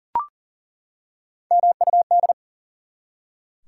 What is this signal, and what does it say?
Morse code, heard as a keyed tone, about 40 wpm, 700 Hz. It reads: MUD
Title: Three Letter Words - 40wpm